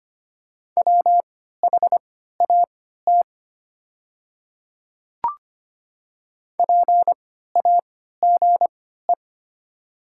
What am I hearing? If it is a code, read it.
Morse code, heard as a keyed tone, about 25 wpm, spaced as for 14 wpm, 700 Hz. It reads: WHAT PAGE
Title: 3x Character Spacing - Top 300 Words - 25wpm